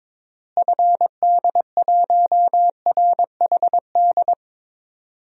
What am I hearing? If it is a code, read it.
Morse code, heard as a keyed tone, about 22 wpm, 700 Hz. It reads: FD1RHD
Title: DXCC-100 Call Signs - 6 Characters - Repeated 2x - 22wpm